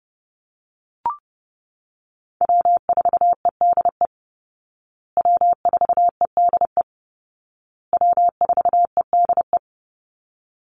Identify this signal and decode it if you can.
Morse code, heard as a keyed tone, about 30 wpm, 700 Hz. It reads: W4EDE W4EDE W4EDE
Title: Call Signs Repeated 2x - 30wpm